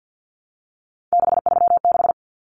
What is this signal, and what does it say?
Morse code, heard as a keyed tone, about 50 wpm, 700 Hz. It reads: 636
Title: Numbers - 3-Digit 50wpm